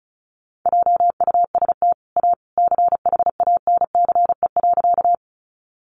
Morse code, heard as a keyed tone, about 34 wpm, 700 Hz. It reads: JUST A CHANCE.
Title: Sentences from Top 800 Words 35wpm